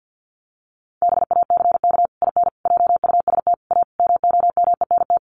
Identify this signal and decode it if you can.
Morse code, heard as a keyed tone, about 50 wpm, 700 Hz. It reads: BACK IN JUST A MOMENT
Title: Sets of 5 Words Encore 50wpm